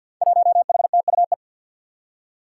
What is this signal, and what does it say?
Morse code, heard as a keyed tone, about 47 wpm, 700 Hz. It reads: 1HTUE
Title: Five Challenging Letter-Number Combinations 50wpm